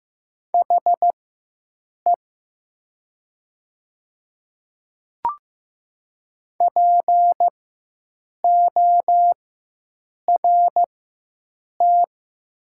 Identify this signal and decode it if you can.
Morse code, heard as a keyed tone, about 15 wpm, spaced as for 7 wpm, 700 Hz. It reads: HE PORT
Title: 4x Character Spacing - Top 300 Words - 15wpm